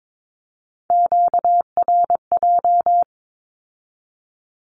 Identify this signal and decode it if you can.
Morse code, heard as a keyed tone, about 22 wpm, 700 Hz. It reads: QRJ
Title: Q-Codes with Spoken Meaning - 22wpm